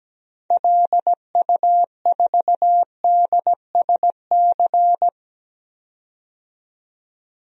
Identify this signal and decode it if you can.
Morse code, heard as a keyed tone, about 17 wpm, 700 Hz. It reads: LU4DSC
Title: DXCC-100 Call Signs - 6 Characters - 17wpm